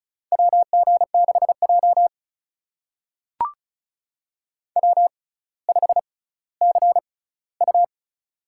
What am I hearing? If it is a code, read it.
Morse code, heard as a keyed tone, about 36 wpm, spaced as for 18 wpm, 700 Hz. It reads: WG6J W5CU
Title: USA Call Signs - 4 Characters - 6x Character Spacing - 35wpm